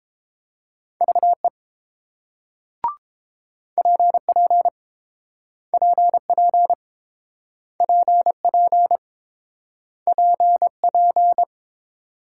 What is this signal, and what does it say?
Morse code, heard as a keyed tone, about 30 wpm, 690 Hz. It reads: VE PP PP PP PP